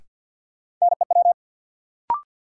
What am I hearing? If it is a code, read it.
Morse code, heard as a keyed tone, about 53 wpm, 700 Hz. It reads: NEW
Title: Sentences from Top 300 Words - New Word Drill - 50wpm